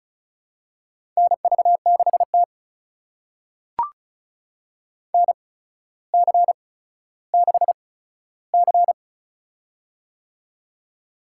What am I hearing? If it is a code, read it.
Morse code, heard as a keyed tone, about 35 wpm, spaced as for 15 wpm, 700 Hz. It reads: NV6T NC6C
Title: USA Call Signs - 4 Characters - 8x Character Spacing - 35wpm